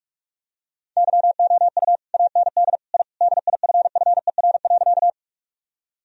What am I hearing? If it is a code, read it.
Morse code, heard as a keyed tone, about 41 wpm, 700 Hz. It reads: YOU AND I DIFFER.
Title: Sentences from Top 1000 Words - Review Part 1 - 45wpm